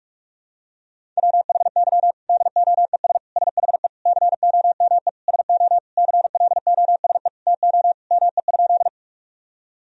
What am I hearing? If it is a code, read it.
Morse code, heard as a keyed tone, about 41 wpm, 700 Hz. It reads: WHY DOES SHE COME SO CLOSE TO ME?
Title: Sentences from Top 1000 Words - Review Part 1 - 45wpm